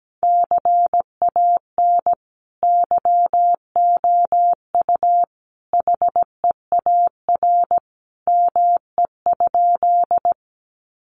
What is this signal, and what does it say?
Morse code, heard as a keyed tone, about 17 wpm, 700 Hz. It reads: CAN YOU HEAR ME?
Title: Sentences from Top 300 Words 17wpm